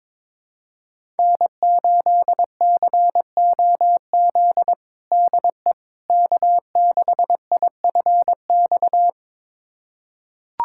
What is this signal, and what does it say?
Morse code, heard as a keyed tone, about 22 wpm, 700 Hz. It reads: N8COZ DE K6IFX
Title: USA Callsign DE Callsign - 5 Characters - 22wpm